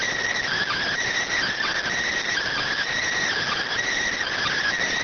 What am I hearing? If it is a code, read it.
Emergency Alert System (SAME) tones (header bursts), heard as harsh buzzing bursts. It is heard from 0.4 to 4.8 s.